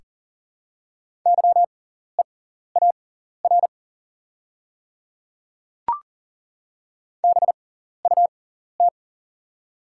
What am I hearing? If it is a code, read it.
Morse code, heard as a keyed tone, about 39 wpm, spaced as for 14 wpm, 690 Hz. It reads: YEAR BUT
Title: 6x Character Spacing - Top 200 Words - 40wpm